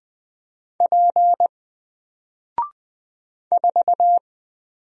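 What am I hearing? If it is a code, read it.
Morse code, heard as a keyed tone, about 20 wpm, 700 Hz. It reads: P 4
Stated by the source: Single letter and number Morse drill at 20 wpm, from N4LCD.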